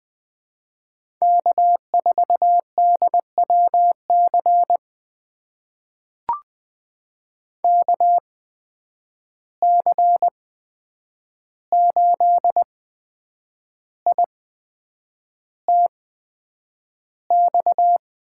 Morse code, heard as a keyed tone, about 20 wpm, spaced as for 8 wpm, 700 Hz. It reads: K4DWC KC8ITX
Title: USA Call Signs - 8x Character Spacing - 20wpm